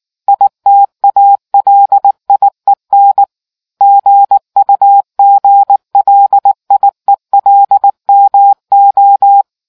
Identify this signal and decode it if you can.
Morse code, heard as a keyed tone, about 19 wpm, 800 Hz. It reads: ITALIEN GUGLIELMO